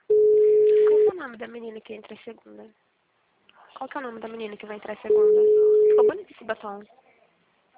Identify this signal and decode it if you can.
Ringback tone (Europe, 425 Hz), with speech.